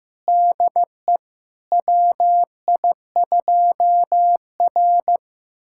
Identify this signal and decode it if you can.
Morse code, heard as a keyed tone, about 15 wpm, 700 Hz. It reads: DE WI2R
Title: USA Callsign DE Callsign - 4 Characters - 15wpm